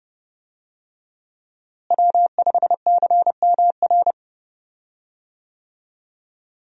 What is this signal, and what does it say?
Morse code, heard as a keyed tone, about 30 wpm, 700 Hz. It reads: W5CMR
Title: USA Call Signs - 5 Characters - Repeated 3x - 30wpm